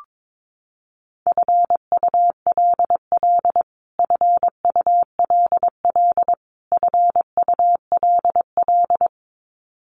Morse code, heard as a keyed tone, about 22 wpm, 700 Hz. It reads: FULL FULL FULL
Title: Top 500 Words Repeated 22wpm